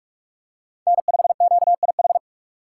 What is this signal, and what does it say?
Morse code, heard as a keyed tone, about 46 wpm, 700 Hz. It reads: N5QIH
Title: USA Call Signs - 5 Characters - 2x Character Spacing - 45wpm